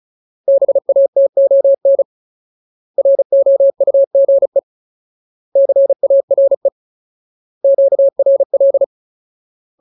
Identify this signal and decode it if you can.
Morse code, heard as a keyed tone, about 34 wpm, 550 Hz. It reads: BATON ROUGE CARE QRL
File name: Ultimate Word List / 35WPM Ultimate